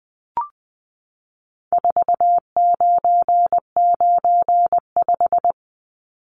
Morse code, heard as a keyed tone, about 20 wpm, 700 Hz. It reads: 4995
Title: Numbers - 4-Digit 20wpm